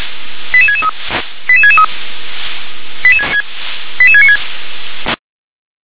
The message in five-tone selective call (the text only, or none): 88663 88663 88696